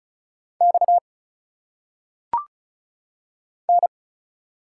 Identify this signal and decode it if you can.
Morse code, heard as a keyed tone, about 35 wpm, 700 Hz. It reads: X N
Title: Single Letters - Phonetic Pronunciation 35wpm